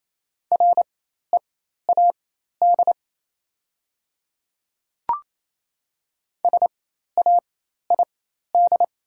Morse code, heard as a keyed tone, about 28 wpm, spaced as for 13 wpm, 700 Hz. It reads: READ SAID
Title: 4x Character Spacing - Top 200 Words - 28wpm